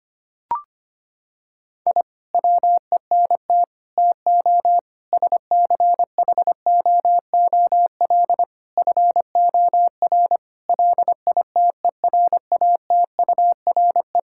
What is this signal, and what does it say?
Morse code, heard as a keyed tone, about 25 wpm, 700 Hz. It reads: I WENT TO SCHOOL FOR LITERATURE
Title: Sentences from Top 1000 Words - Review Part 6 - 25wpm